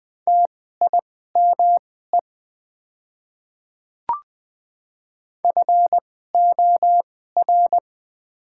Morse code, heard as a keyed tone, about 20 wpm, spaced as for 15 wpm, 690 Hz. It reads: TIME FOR
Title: 2x Character Spacing - Top 100 Words - 20wpm